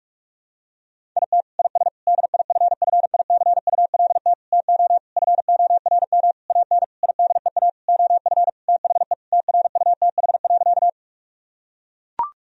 Morse code, heard as a keyed tone, about 41 wpm, 700 Hz. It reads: IT IS DIFFICULT TO FORM AN IDEA OF THE TRUTH.
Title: Sentences from Top 1000 Words - Review Part 1 - 45wpm